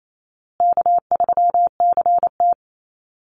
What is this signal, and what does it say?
Morse code, heard as a keyed tone, about 28 wpm, 700 Hz. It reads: K3CT
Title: Call Signs Repeated 3x - 28wpm